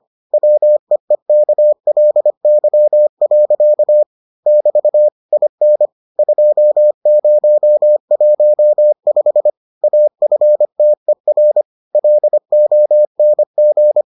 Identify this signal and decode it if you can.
Morse code, heard as a keyed tone, about 25 wpm, 600 Hz. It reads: WEEKLY. <BT> IN 2015 AFTER LONG